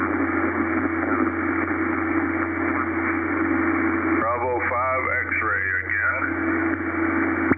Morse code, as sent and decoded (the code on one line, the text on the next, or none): none